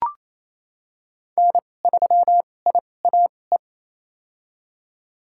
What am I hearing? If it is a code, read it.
Morse code, heard as a keyed tone, about 28 wpm, spaced as for 20 wpm, 700 Hz. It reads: N3IAE
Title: USA Call Signs - 2x Character Spacing - 28wpm